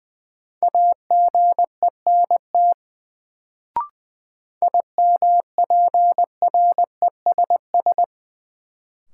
Morse code, heard as a keyed tone, about 20 wpm, 700 Hz. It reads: AGENT IMPRESS
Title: Sentences from Top 1600 Words - New Word Drill - 20wpm